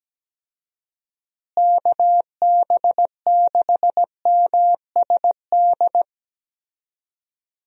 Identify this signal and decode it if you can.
Morse code, heard as a keyed tone, about 17 wpm, 700 Hz. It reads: KB6MSD
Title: USA Call Signs - 6 Characters - Repeated 3x - 17wpm